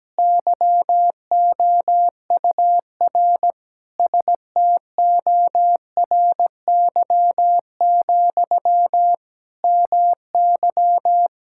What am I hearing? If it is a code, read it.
Morse code, heard as a keyed tone, about 17 wpm, 700 Hz. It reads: YOUR STORY, MY
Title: Sentences from Top 1000 Words - Review Part 6 - 17wpm